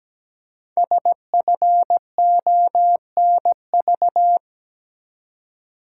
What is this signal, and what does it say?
Morse code, heard as a keyed tone, about 17 wpm, 700 Hz. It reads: SFONV
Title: Code Groups - 5 Letters - 17wpm